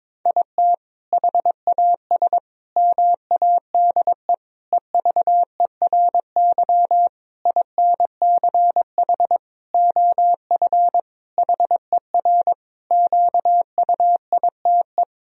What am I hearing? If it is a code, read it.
Morse code, heard as a keyed tone, about 22 wpm, 700 Hz. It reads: IT HAS MADE EVERY INCH OF HER QUITE
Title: Sentences from Top 1000 Words - Review Part 3 - 22wpm